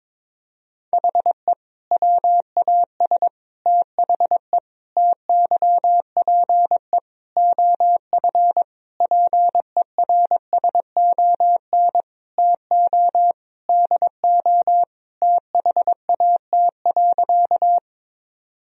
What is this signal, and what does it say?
Morse code, heard as a keyed tone, about 22 wpm, 700 Hz. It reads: HE WAS THE TYPE OF PERSON TO DO THAT.
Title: Sentences from Top 1000 Words - Review Part 3 - 22wpm